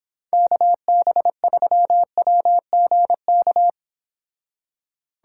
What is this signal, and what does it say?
Morse code, heard as a keyed tone, about 29 wpm, 700 Hz. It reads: KB3WGK